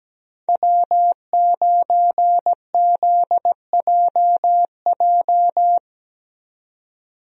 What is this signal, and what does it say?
Morse code, heard as a keyed tone, about 17 wpm, 700 Hz. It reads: W9ZJJ
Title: USA Call Signs - 5 Characters - 17wpm